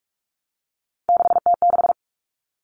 Morse code, heard as a keyed tone, about 44 wpm, 700 Hz. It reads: 6T6
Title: All Cut Numbers - 3-Digit 45wpm